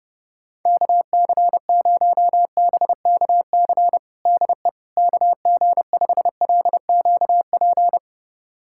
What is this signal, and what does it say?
Morse code, heard as a keyed tone, about 30 wpm, 700 Hz. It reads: KC0BKC DE KG5LQP
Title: USA Callsign DE Callsign - 6 Characters - 30wpm